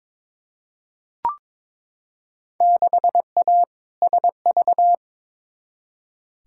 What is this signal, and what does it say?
Morse code, heard as a keyed tone, about 21 wpm, 700 Hz. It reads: 6A SV
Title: Field Day Exchanges Encore - 22wpm